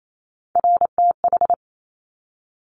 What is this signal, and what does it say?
Morse code, heard as a keyed tone, about 28 wpm, 700 Hz. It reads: RTH
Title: Top Three Letter Combinations in Words 28wpm